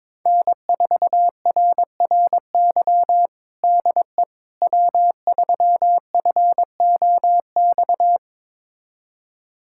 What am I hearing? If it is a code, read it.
Morse code, heard as a keyed tone, about 22 wpm, 700 Hz. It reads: N4RRY DE W3FOX
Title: USA Callsign DE Callsign - 5 Characters - 22wpm